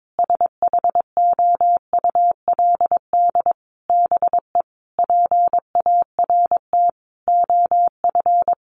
Morse code, heard as a keyed tone, about 22 wpm, 700 Hz. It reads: SHOULD BE PART OF
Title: Sentences from Top 1500 Words 22wpm